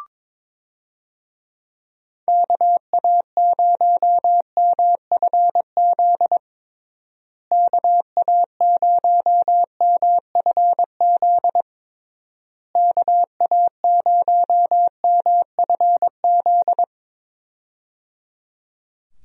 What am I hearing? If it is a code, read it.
Morse code, heard as a keyed tone, about 22 wpm, 700 Hz. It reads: KA0MFZ KA0MFZ KA0MFZ
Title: USA Call Signs - 6 Characters - Repeated 3x - 22wpm